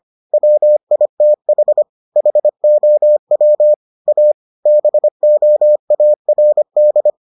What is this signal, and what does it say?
Morse code, heard as a keyed tone, about 25 wpm, 600 Hz. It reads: WITH HOW A BOARD